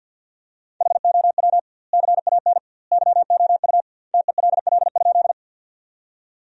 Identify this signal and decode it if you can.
Morse code, heard as a keyed tone, about 50 wpm, 700 Hz. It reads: HOW CAN YOU TELL?
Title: 2x Word-Spacing - Sentences from Top 200 Words - 50wpm